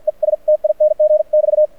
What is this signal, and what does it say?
Morse code, heard as a keyed tone, about 50 wpm, spaced as for 43 wpm, 600 Hz. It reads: ESTING <BT>
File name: archived / CRSnet